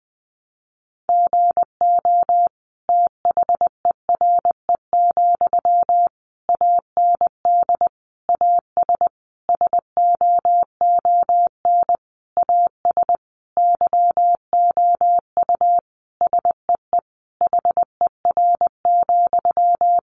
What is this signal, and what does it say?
Morse code, heard as a keyed tone, about 20 wpm, 700 Hz. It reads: GO THERE, AND AS SOON AS YOU SEE HER,
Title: Sentences from Top 1300 Words 20wpm